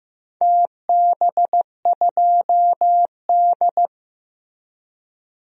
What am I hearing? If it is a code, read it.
Morse code, heard as a keyed tone, about 15 wpm, 700 Hz. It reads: TB2D
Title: DXCC-100 Call Signs - 4 Characters - Repeated 3x - 15wpm